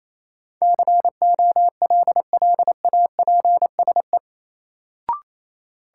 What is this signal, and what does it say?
Morse code, heard as a keyed tone, about 28 wpm, 700 Hz. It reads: COLLAPSE
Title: Sets of 5 Words Encore - New Word Drill - 28wpm